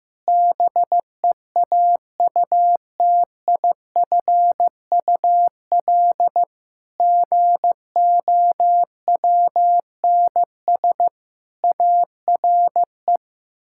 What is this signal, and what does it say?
Morse code, heard as a keyed tone, about 16 wpm, 700 Hz. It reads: BEAUTIFUL GOWNS ARE
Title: Pangram Challenge v2 - 15wpm